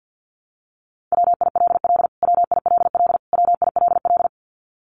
Morse code, heard as a keyed tone, about 50 wpm, 700 Hz. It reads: WILL WILL WILL
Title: Top 500 Words Repeated 50wpm